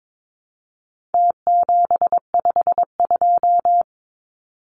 Morse code, heard as a keyed tone, about 22 wpm, 700 Hz. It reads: T752
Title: Numbers - Contest Style - 4-Digit 22wpm